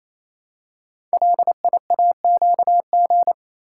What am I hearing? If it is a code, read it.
Morse code, heard as a keyed tone, about 27 wpm, 700 Hz. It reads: LIAQG
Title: Code Groups - 5 Letters - 28wpm